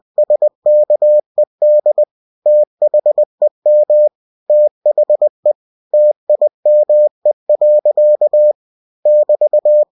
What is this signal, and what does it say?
Morse code, heard as a keyed tone, about 20 wpm, 600 Hz. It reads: SKED THEM THE TIME. <BT>
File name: cw-books / winnie-the-pooh / winnie-20wpm-page-0106